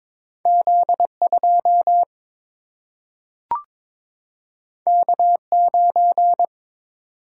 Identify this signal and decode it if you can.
Morse code, heard as a keyed tone, about 22 wpm, 700 Hz. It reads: Z2 K9
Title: Single-Letter Single-Number 22wpm